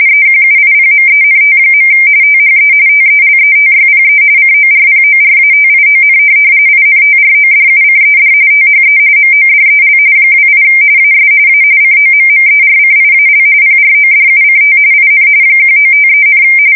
RTTY, about 45.45 baud. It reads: EQUIPMENT. THE TELETYPE CORPORATION INTRODUCED THE MODEL 28 AS A COMMERCIAL PRODUCT IN 1953,
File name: fsk170 / teletype-corp